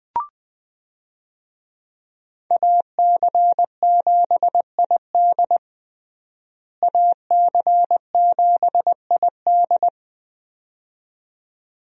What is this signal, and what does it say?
Morse code, heard as a keyed tone, about 21 wpm, 700 Hz. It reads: AC7ID AC7ID
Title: USA Call Signs - 5 Characters - Repeated 2x - 20wpm